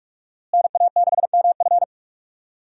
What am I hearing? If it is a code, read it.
Morse code, heard as a keyed tone, about 42 wpm, 700 Hz. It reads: NA6MF
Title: USA Call Signs - 5 Characters - Repeated 2x - 45wpm